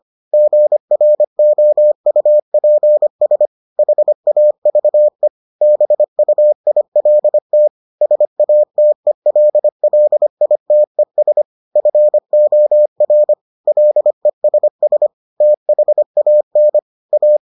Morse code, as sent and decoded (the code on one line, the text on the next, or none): --. .-. --- ..- .--. ... / .... .- ...- . / -... ..- .. .-.. - / ... .- - . .-.. .-.. .. - . ... / ..-. --- .-. / .-.. ... ... / - .... .- -. / .-
GROUPS HAVE BUILT SATELLITES FOR LESS THAN A